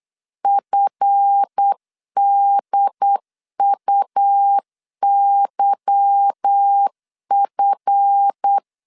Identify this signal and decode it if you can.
Morse code, heard as a keyed tone, about 8 wpm, 800 Hz. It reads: FDUYF
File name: LEZIONI / 8 YJ45 / 8 Y 8wpm 3245